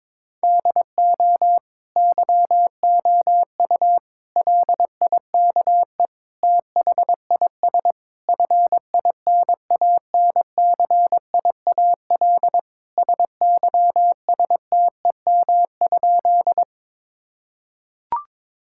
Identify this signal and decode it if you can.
Morse code, heard as a keyed tone, about 22 wpm, 700 Hz. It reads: DO YOU LIKE THIS FINANCIAL SYSTEM?
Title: Sentences from Top 1000 Words - Review Part 5 - 22wpm